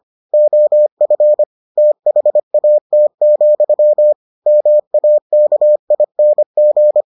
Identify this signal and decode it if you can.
Morse code, heard as a keyed tone, about 25 wpm, 600 Hz. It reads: OF THAT, MAKING